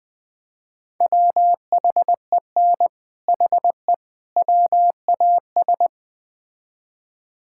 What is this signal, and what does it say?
Morse code, heard as a keyed tone, about 21 wpm, 700 Hz. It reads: WHEN HE WAS
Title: Build A Phrase - 3-Word Phrases - 20wpm